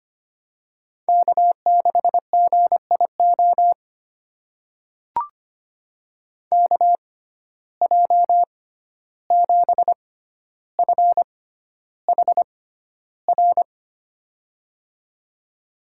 Morse code, heard as a keyed tone, about 25 wpm, spaced as for 12 wpm, 700 Hz. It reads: K6GIO KJ7FHR
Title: USA Call Signs - 6x Character Spacing - 25wpm